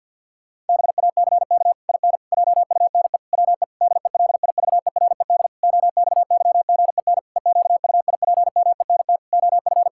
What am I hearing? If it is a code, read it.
Morse code, heard as a keyed tone, about 47 wpm, 700 Hz. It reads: BACK IN JUNE WE DELIVERED OXYGEN EQUIPMENT OF
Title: Pangram Challenge v2 - 50wpm